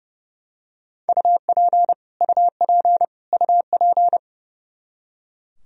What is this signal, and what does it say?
Morse code, heard as a keyed tone, about 30 wpm, 700 Hz. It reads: UP UP UP
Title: Top 100 Words Repeated 30wpm